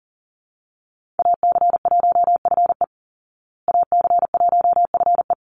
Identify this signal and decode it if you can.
Morse code, heard as a keyed tone, about 40 wpm, 700 Hz. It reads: AC1FE AC1FE
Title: Call Signs Repeated 3x - 40wpm